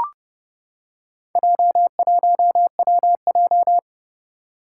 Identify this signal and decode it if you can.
Morse code, heard as a keyed tone, about 30 wpm, 700 Hz. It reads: J1WJ
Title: Four Challenging Letter-Number Combinations 30wpm